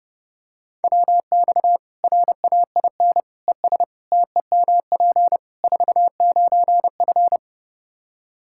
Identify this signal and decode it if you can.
Morse code, heard as a keyed tone, about 32 wpm, 700 Hz. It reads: WX RAIN ES TEMP 49F
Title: Practice QSO - Exchange 2 - Info - 30wpm